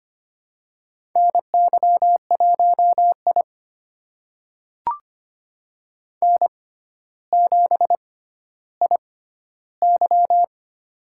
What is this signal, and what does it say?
Morse code, heard as a keyed tone, about 25 wpm, spaced as for 13 wpm, 700 Hz. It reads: NY1I N7IY
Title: USA Call Signs - 4 Characters - 6x Character Spacing - 25wpm